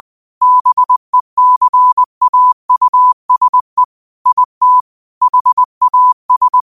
Morse code, heard as a keyed tone, about 20 wpm, 1000 Hz. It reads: BECAUSE IT HAS
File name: 20wpm